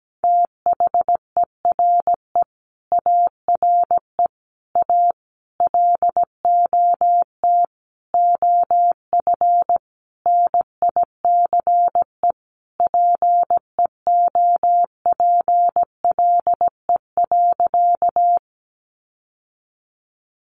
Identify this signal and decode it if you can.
Morse code, heard as a keyed tone, about 17 wpm, 700 Hz. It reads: THERE ARE A LOT OF NICE PEOPLE.
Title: Sentences from Top 1100 Words 17wpm